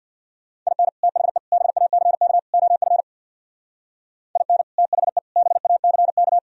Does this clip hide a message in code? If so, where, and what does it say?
Yes: Morse code from 0.7 to 6.4 s, heard as a keyed tone, about 64 wpm, 700 Hz. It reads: IN THE BACK OF IN THE BACK